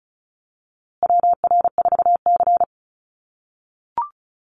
Morse code, heard as a keyed tone, about 34 wpm, 700 Hz. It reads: WR4C
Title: USA Call Signs Repeated 3x 35wpm